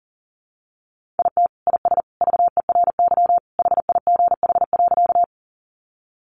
Morse code, heard as a keyed tone, about 40 wpm, 700 Hz. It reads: IT IS VERY HIGH.
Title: Sentences from Top 200 Words 40wpm